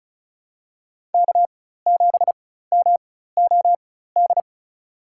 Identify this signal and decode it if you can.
Morse code, heard as a keyed tone, about 34 wpm, spaced as for 16 wpm, 700 Hz. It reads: K7MOD
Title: USA Call Signs - 4x Character Spacing - 35wpm